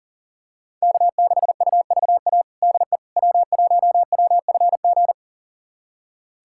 Morse code, heard as a keyed tone, about 40 wpm, 700 Hz. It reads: K6UVA DE W1WFG
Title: USA Callsign DE Callsign - 5 Characters - 40wpm